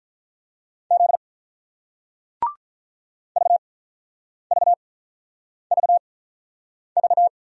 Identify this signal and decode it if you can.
Morse code, heard as a keyed tone, about 48 wpm, spaced as for 9 wpm, 700 Hz. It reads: Z VVVV